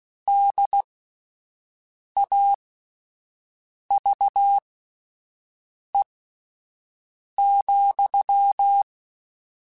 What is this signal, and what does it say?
Morse code, heard as a keyed tone, about 16 wpm, spaced as for 5 wpm, 800 Hz. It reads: DAVE,
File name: QSOs Spaced